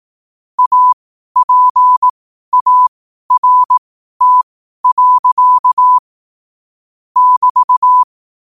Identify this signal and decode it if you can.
Morse code, heard as a keyed tone, about 18 wpm, spaced as for 12 wpm, 1000 Hz. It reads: APART. <BT>
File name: qotdfiles / 12wpm